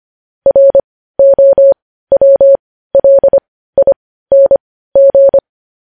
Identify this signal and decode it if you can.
Morse code, heard as a keyed tone, about 25 wpm, spaced as for 15 wpm, 550 Hz. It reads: ROWLING